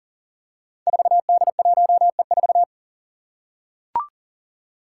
Morse code, heard as a keyed tone, about 37 wpm, 700 Hz. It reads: 4D1E4
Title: Five Challenging Letter-Number Combinations 40wpm